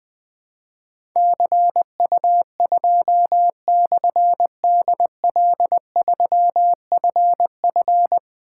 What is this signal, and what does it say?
Morse code, heard as a keyed tone, about 20 wpm, 700 Hz. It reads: CU2/DL3FF
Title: Difficult Call Signs 20wpm